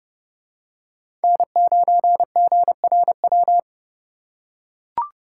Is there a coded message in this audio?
Yes: Morse code from 1.2 to 3.6 s, heard as a keyed tone, about 28 wpm, 700 Hz. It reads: N9GRW